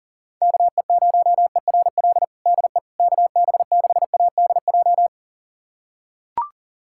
Morse code, heard as a keyed tone, about 37 wpm, 700 Hz. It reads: KE0ERL DE KB6ADJ
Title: USA Callsign DE Callsign 40wpm